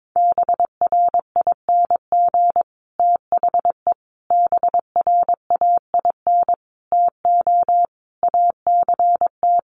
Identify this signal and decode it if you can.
Morse code, heard as a keyed tone, about 22 wpm, 700 Hz. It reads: BRING THE BRAIN TO ACT
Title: Sentences from Top 1200 Words 22wpm